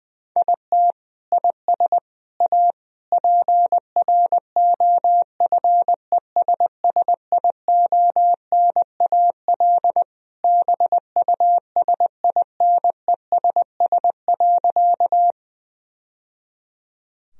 Morse code, heard as a keyed tone, about 20 wpm, 700 Hz. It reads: IT IS A PROFESSIONAL BUSINESS.
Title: Sentences from Top 1000 Words - Review Part 4 - 20wpm